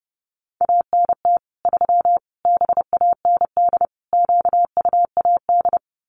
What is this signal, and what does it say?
Morse code, heard as a keyed tone, about 30 wpm, 700 Hz. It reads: ANT 3 BAND QUAD
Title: Common QSO Phrases 30wpm